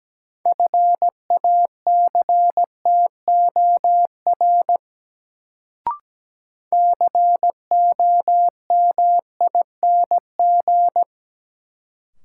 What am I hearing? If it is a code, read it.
Morse code, heard as a keyed tone, about 17 wpm, 700 Hz. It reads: FACTOR COMING
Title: Sentences from Top 1300 Words - New Word Drill Alt - 17wpm